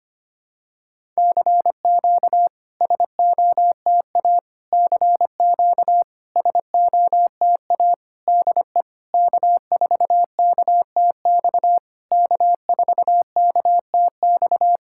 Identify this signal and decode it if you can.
Morse code, heard as a keyed tone, about 25 wpm, 700 Hz. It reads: CQ SOTA CQ SOTA DE K4KTX K4KTX
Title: SOTA QSO Practice 25wpm